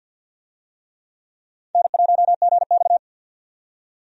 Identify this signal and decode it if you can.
Morse code, heard as a keyed tone, about 47 wpm, 700 Hz. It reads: N1GX
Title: USA Call Signs - 4 Characters - 50wpm